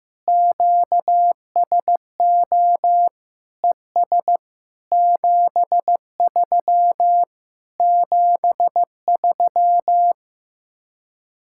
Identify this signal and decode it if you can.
Morse code, heard as a keyed tone, about 15 wpm, 700 Hz. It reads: QSO ES 73 73
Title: Speed-Racing Common QSO Phrases Encore - 15wpm